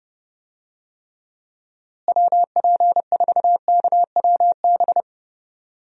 Morse code, heard as a keyed tone, about 30 wpm, 700 Hz. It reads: WP4KWB
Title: USA Call Signs - 6 Characters - Repeated 3x - 30wpm